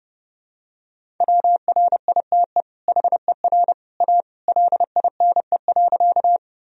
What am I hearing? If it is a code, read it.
Morse code, heard as a keyed tone, about 30 wpm, 700 Hz. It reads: WRITE HER A LINE.
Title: Sentences from Top 1000 Words - Review Part 1 - 30wpm